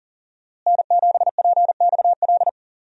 Morse code, heard as a keyed tone, about 40 wpm, 700 Hz. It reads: N7PXL
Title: USA Call Signs - 5 Characters - Repeated 3x - 40wpm